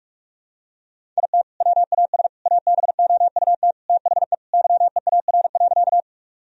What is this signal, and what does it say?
Morse code, heard as a keyed tone, about 46 wpm, 700 Hz. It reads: IT WAS ABOUT THE YEAR.
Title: Sentences from Top 1000 Words - Review Part 1 - 45wpm